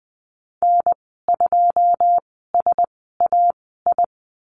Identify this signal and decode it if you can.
Morse code, heard as a keyed tone, about 20 wpm, spaced as for 14 wpm, 700 Hz. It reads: N2SAI
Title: 128 Lesson 21g - Number 6 - Call Signs 20wpm